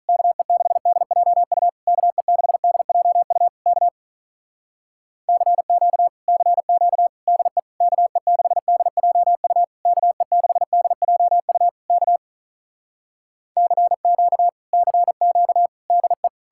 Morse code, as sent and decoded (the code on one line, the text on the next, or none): -.- . -.... -.. .--- ..- / -.- . -.... -.. .--- ..- / -.- / -.-. --.- / -.-. --.- / -.. / -.- . -.... -.. .--- ..- / -.- . -.... -.. .--- ..- / -.- / -.-. --.- / -.-. --.- / -.. .
KE6DJU KE6DJU K CQ CQ DE KE6DJU KE6DJU K CQ CQ DE